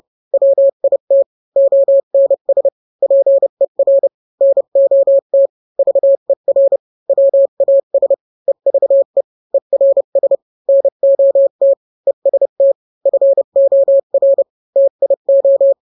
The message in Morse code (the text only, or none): WIT ONS PER NOT VER WAS EVE ERS NOT EST FOR TIO